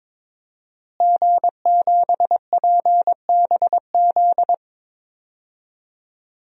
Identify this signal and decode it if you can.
Morse code, heard as a keyed tone, about 22 wpm, 700 Hz. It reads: G7PBZ
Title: DXCC-100 Call Signs - 5 Characters - Repeated 3x - 22wpm